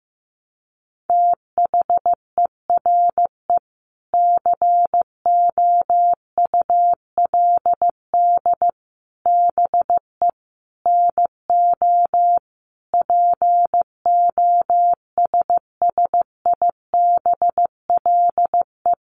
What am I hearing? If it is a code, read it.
Morse code, heard as a keyed tone, about 15 wpm, 700 Hz. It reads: THERE COULD BE NO POSSIBLE
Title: Sentences from Top 700 Words 15wpm